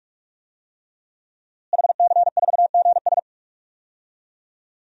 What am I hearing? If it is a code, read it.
Morse code, heard as a keyed tone, about 46 wpm, 700 Hz. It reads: HC4GS
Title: DXCC-100 Call Signs - 5 Characters - Repeated 3x - 45wpm